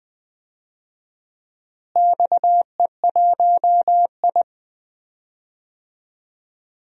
Morse code, heard as a keyed tone, about 20 wpm, 700 Hz. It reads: XE1I
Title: DXCC-100 Call Signs - 4 Characters - Repeated 3x - 20wpm